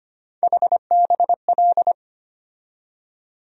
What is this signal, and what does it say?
Morse code, heard as a keyed tone, about 25 wpm, 700 Hz. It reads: HBL
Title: USA Call Sign Postfixes - 3 Letters 25wpm